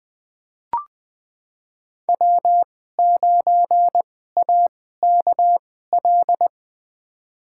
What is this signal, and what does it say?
Morse code, heard as a keyed tone, about 20 wpm, spaced as for 14 wpm, 700 Hz. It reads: W9AKL
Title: USA Call Signs - 2x Character Spacing - 20wpm